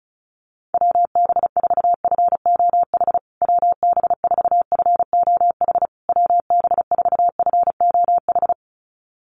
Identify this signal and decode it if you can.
Morse code, heard as a keyed tone, about 34 wpm, 700 Hz. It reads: WB4FOH WB4FOH WB4FOH
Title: USA Call Signs Repeated 3x 35wpm